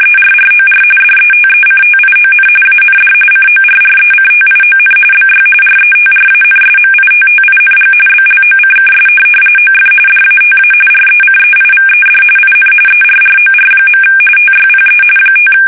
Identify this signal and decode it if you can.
RTTY, about 45.45 baud. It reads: ERROR DETECTION AND CORRECTION BY RETRANSMITTING BLOCKS OF DATA RECEIVED IN ERROR. TYPE 5
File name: fsk850 / teletype-corp